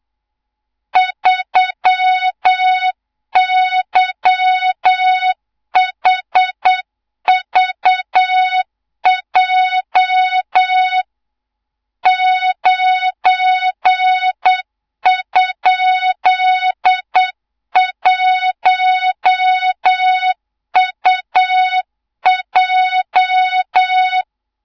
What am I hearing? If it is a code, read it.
Morse code, heard as a keyed tone, about 8 wpm, 3700 Hz, with noise. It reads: 3YHVJ 9?1UJ
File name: Morze-HA2MN / hang / 24 GYAKORLAT